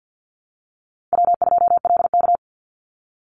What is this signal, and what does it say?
Morse code, heard as a keyed tone, about 50 wpm, 700 Hz. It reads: W2LK
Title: Easy Call Signs Repeated 2x - 50wpm